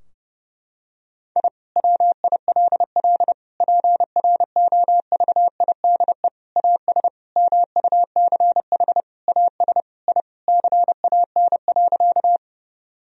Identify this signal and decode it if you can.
Morse code, heard as a keyed tone, about 30 wpm, 700 Hz. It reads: I WILL PROVIDE AS MUCH AS I CAN.
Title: Sentences from Top 1000 Words - Review Part 3 - 30wpm